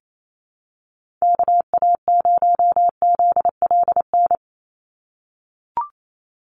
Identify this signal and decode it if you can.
Morse code, heard as a keyed tone, about 28 wpm, 700 Hz. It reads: KA0ZLN
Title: USA Call Signs 28wpm